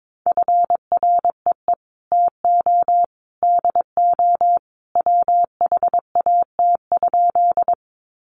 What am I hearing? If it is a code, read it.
Morse code, heard as a keyed tone, about 22 wpm, 700 Hz. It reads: FREE TO DO WHAT?
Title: Sentences from Top 500 Words 22wpm